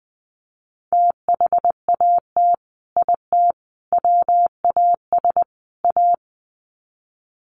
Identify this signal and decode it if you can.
Morse code, heard as a keyed tone, about 20 wpm, 700 Hz. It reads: THAT IT WAS A
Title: Sets of 4 Words 20wpm